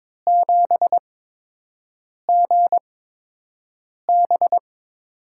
Morse code, heard as a keyed tone, about 21 wpm, spaced as for 6 wpm, 700 Hz. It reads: 7GB
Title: USA Call Signs - 6 Characters - 8x Character Spacing - 22wpm